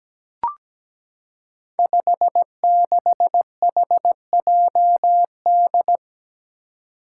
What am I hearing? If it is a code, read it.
Morse code, heard as a keyed tone, about 17 wpm, 700 Hz. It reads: 56HJD
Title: Five Challenging Letter-Number Combinations 17wpm